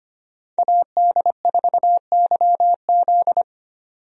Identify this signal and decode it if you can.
Morse code, heard as a keyed tone, about 25 wpm, 690 Hz. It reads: AD4YZ
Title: USA Call Signs - 5 Characters - 25wpm